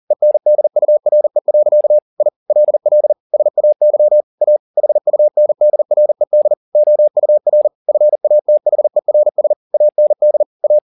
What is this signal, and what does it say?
Morse code, heard as a keyed tone, about 40 wpm, 600 Hz. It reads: ENDURE. I LL SAY A HUNDRED OUR FATHERS AND A
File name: the old man and the sea 2 40wpm